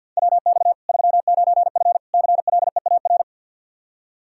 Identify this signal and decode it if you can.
Morse code, heard as a keyed tone, about 51 wpm, 700 Hz. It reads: WX 39F CLEAR
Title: Practice QSO - Exchange 2 - Info - 50wpm